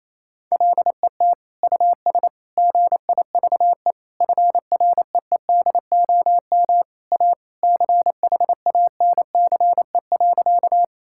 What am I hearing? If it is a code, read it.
Morse code, heard as a keyed tone, about 28 wpm, 700 Hz. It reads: LET US GIVE FREEDOM A CHANCE.
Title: Sentences from Top 1000 Words - Review Part 5 - 28wpm